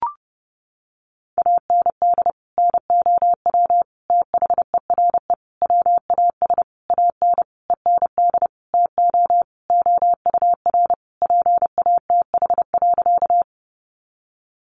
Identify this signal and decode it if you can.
Morse code, heard as a keyed tone, about 30 wpm, 700 Hz. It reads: AND NOW THERE WAS AN END TO OUR PATH.
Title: Sentences from Top 1400 Words 30wpm